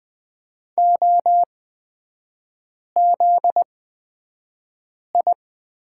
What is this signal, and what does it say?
Morse code, heard as a keyed tone, about 20 wpm, spaced as for 5 wpm, 700 Hz. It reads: OZI
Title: 243 Lesson 39e - Letter z - Three Characters 20wpm